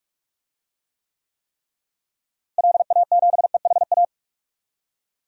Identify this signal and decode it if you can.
Morse code, heard as a keyed tone, about 42 wpm, 700 Hz. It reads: LA7EHA
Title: DXCC-100 Call Signs - 6 Characters - Repeated 3x - 45wpm